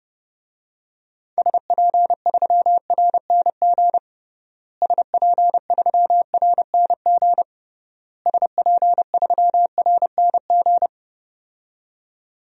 Morse code, heard as a keyed tone, about 30 wpm, 700 Hz. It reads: SP3RNG SP3RNG SP3RNG
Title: DXCC-100 Call Signs - 6 Characters - Repeated 3x - 30wpm